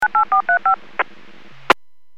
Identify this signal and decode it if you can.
DTMF, heard as telephone touch tones, tones of about 110 ms. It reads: 98435